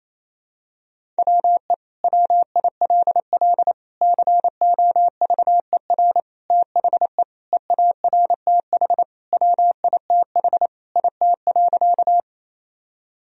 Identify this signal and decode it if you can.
Morse code, heard as a keyed tone, about 28 wpm, 700 Hz. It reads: WE WILL COVER THE EARTH WITH IT.
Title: Sentences from Top 1000 Words - Review Part 1 - 28wpm